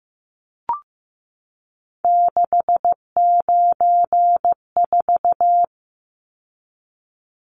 Morse code, heard as a keyed tone, about 15 wpm, 700 Hz. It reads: 694
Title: Numbers - 3-Digit 15wpm